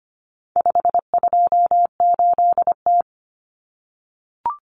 Morse code, heard as a keyed tone, about 25 wpm, 700 Hz. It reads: 528T
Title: Common Cut Numbers - 4-Digit 25wpm